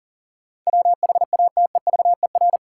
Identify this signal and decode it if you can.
Morse code, heard as a keyed tone, about 40 wpm, 700 Hz. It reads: WHATEVER
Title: Sentences from Top 1100 Words - New Word Drill Alt - 40wpm